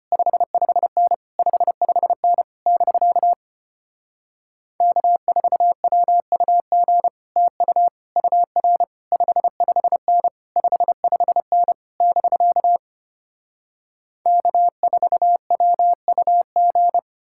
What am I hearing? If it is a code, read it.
Morse code, heard as a keyed tone, about 33 wpm, 700 Hz. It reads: 55N 55N <BK> K4WUG TU UR 55N 55N <BK> K4WUG